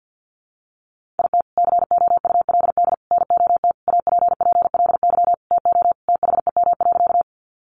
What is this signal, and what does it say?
Morse code, heard as a keyed tone, about 50 wpm, 700 Hz. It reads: IT COULD NOT APPLY TO THEM.
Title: Sentences from Top 1200 Words 50wpm